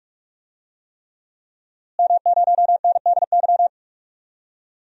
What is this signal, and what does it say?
Morse code, heard as a keyed tone, about 43 wpm, 700 Hz. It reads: M0NDY
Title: DXCC-100 Call Signs - 5 Characters - Repeated 3x - 45wpm